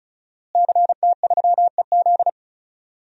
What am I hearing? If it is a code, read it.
Morse code, heard as a keyed tone, about 36 wpm, 700 Hz. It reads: CT3EZ
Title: DXCC-100 Call Signs - 5 Characters - Repeated 3x - 35wpm